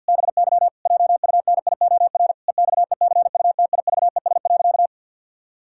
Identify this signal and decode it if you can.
Morse code, heard as a keyed tone, about 51 wpm, 700 Hz. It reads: BY JUNIOR EXECUTIVES.
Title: Pangram Challenge v2 - 50wpm